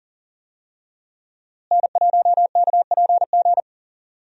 Morse code, heard as a keyed tone, about 40 wpm, 700 Hz. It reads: N1KPG
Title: USA Call Signs - 5 Characters - Repeated 2x - 40wpm